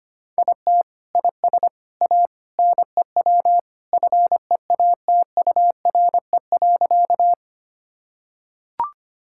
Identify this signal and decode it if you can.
Morse code, heard as a keyed tone, about 25 wpm, 700 Hz. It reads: IT IS A NEW FEATURE.
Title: Sentences from Top 1000 Words - Review Part 5 - 25wpm